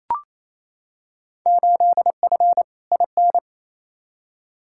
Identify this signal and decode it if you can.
Morse code, heard as a keyed tone, about 28 wpm, 700 Hz. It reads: 8F IN
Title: Field Day Exchanges Encore - 28wpm